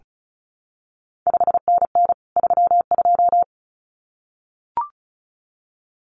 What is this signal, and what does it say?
Morse code, heard as a keyed tone, about 36 wpm, 700 Hz. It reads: 5NN 32
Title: CQ WW DX Contest Exchange Encore - 35wpm